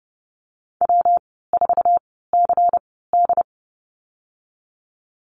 Morse code, heard as a keyed tone, about 30 wpm, 700 Hz. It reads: W 4 C D
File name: Lesson 19g - Letter w - Call Signs 30wpm